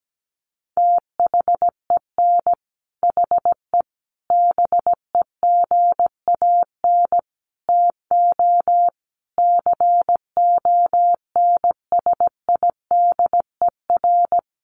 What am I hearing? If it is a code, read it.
Morse code, heard as a keyed tone, about 17 wpm, 700 Hz. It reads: THEN HE BEGAN TO CONSIDER
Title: Sentences from Top 1200 Words 17wpm